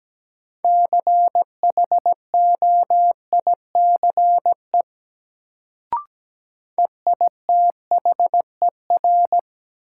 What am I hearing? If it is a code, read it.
Morse code, heard as a keyed tone, about 17 wpm, 700 Hz. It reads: CHOICE EITHER
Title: Sets of 5 Words - New Word Drill - 17wpm